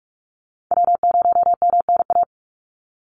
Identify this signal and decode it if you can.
Morse code, heard as a keyed tone, about 44 wpm, 700 Hz. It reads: W0MNA
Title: NA SOTA Chaser Callsigns 45wpm